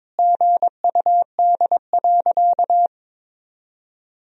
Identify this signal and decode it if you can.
Morse code, heard as a keyed tone, about 22 wpm, 700 Hz. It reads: GUD.
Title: Sentences from Top 200 Words and Abbreviations 22wpm